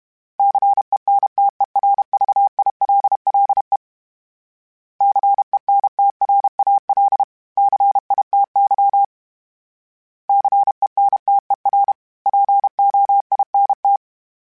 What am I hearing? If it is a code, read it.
Morse code, heard as a keyed tone, about 32 wpm, 800 Hz. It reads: CENTERVILLE CENTRAL CITY CENTER POINT
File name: fromSMs / FromSMs mp3s / Cities@30